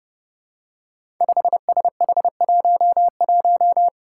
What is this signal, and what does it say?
Morse code, heard as a keyed tone, about 30 wpm, 700 Hz. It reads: 5SH11
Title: Five Challenging Letter-Number Combinations 30wpm